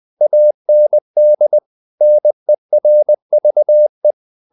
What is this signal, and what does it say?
Morse code, heard as a keyed tone, about 20 wpm, 600 Hz. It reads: AND NERVE
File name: CW / Practice / If20